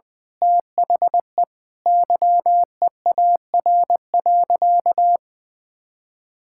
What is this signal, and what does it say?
Morse code, heard as a keyed tone, about 20 wpm, 700 Hz. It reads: THE YEAR.
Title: Sentences with 8-Word Phrases - Spoken Beginning - 20wpm